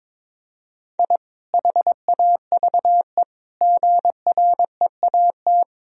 Morse code, heard as a keyed tone, about 21 wpm, 700 Hz. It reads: I HAVE GREAT
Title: Sentences from Top 1000 Words - Review Part 2 - 22wpm